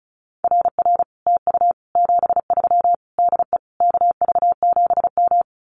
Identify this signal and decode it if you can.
Morse code, heard as a keyed tone, about 36 wpm, 700 Hz. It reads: RR TU 73 DE KV7M
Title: Common QSO Phrases Encore 35wpm